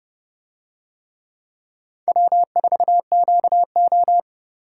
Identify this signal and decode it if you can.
Morse code, heard as a keyed tone, about 28 wpm, 700 Hz. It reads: W4QO